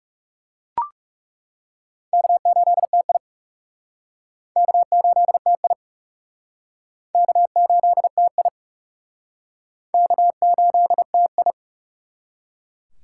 Morse code, heard as a keyed tone, about 39 wpm, 700 Hz. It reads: K8TI K8TI K8TI K8TI